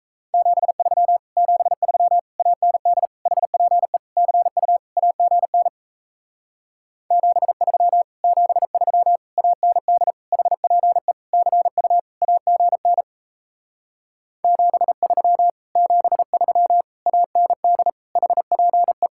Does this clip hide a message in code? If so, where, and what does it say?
Yes: Morse code from 0.3 to 19.1 s, heard as a keyed tone, about 37 wpm, 700 Hz. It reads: 73 73 AND HPE CU AGN 73 73 AND HPE CU AGN 73 73 AND HPE